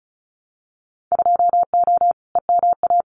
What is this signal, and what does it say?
Morse code, heard as a keyed tone, about 36 wpm, 700 Hz. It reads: 2O EMA
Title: Winter Field Day Exchanges Encore 35wpm